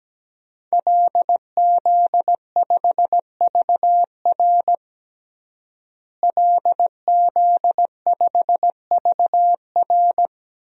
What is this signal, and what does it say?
Morse code, heard as a keyed tone, about 17 wpm, 700 Hz. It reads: LZ5VR LZ5VR
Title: DXCC-100 Call Signs - 5 Characters - Repeated 2x - 17wpm